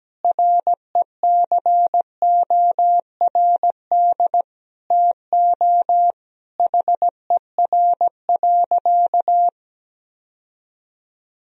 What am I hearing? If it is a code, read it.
Morse code, heard as a keyed tone, about 17 wpm, 700 Hz. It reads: RECORD TO HER.
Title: Sentences from Top 1000 Words - Review Part 3 - 17wpm